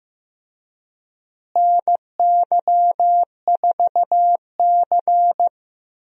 Morse code, heard as a keyed tone, about 16 wpm, 700 Hz. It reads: NY4C
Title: USA Call Signs - 4 Characters - Repeated 3x - 15wpm